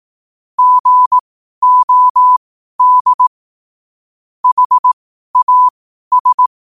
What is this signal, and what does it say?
Morse code, heard as a keyed tone, about 18 wpm, spaced as for 12 wpm, 1000 Hz. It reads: GOD HAS
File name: qotdfiles / 12wpm